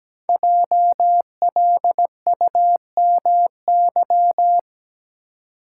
Morse code, heard as a keyed tone, about 17 wpm, 700 Hz. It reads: JLUMY
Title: Code Groups - 5 Letters - 17wpm